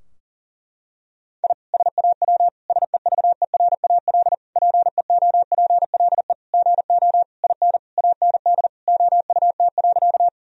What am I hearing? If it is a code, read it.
Morse code, heard as a keyed tone, about 40 wpm, 700 Hz. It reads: I SAW SEVERAL PEOPLE GO IN AND OUT.
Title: Sentences from Top 1000 Words - Review Part 2 - 40wpm